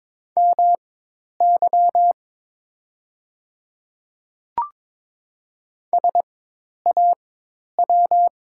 Morse code, heard as a keyed tone, about 22 wpm, spaced as for 10 wpm, 690 Hz. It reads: MY SAW
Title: 4x Character Spacing - Top 300 Words - 22wpm